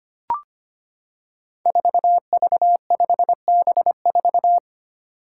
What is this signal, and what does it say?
Morse code, heard as a keyed tone, about 25 wpm, 700 Hz. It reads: 4V5B4
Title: Five Challenging Letter-Number Combinations 25wpm